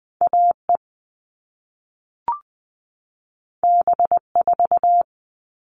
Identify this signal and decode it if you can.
Morse code, heard as a keyed tone, about 20 wpm, 700 Hz. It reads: AE B4
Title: All Cut Numbers - 2-Digit 20wpm